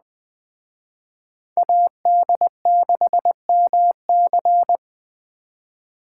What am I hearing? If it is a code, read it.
Morse code, heard as a keyed tone, about 20 wpm, 700 Hz. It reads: AD6MC
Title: Speed Racing - USA Call Signs 17wpm